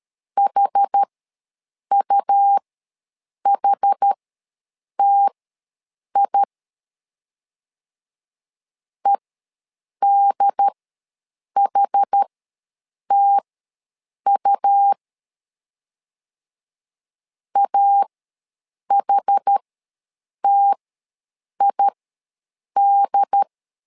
Morse code, heard as a keyed tone, about 13 wpm, spaced as for 7 wpm, 800 Hz. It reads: HUHTI EDHTU AHTID
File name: LEZIONI / 2 UHDN / 2 UHD ws 12wpm 1722